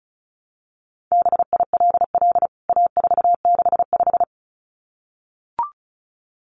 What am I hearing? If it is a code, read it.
Morse code, heard as a keyed tone, about 34 wpm, 700 Hz. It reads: BILL A465
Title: CWT Exchange 35wpm